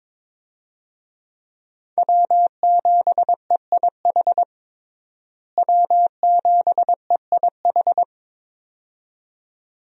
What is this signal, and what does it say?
Morse code, heard as a keyed tone, about 22 wpm, 700 Hz. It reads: W7EIH W7EIH
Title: USA Call Signs - 5 Characters - Repeated 2x - 22wpm